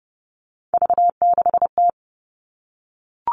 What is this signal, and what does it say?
Morse code, heard as a keyed tone, about 29 wpm, 700 Hz. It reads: V6T
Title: All Cut Numbers - 3-Digit 30wpm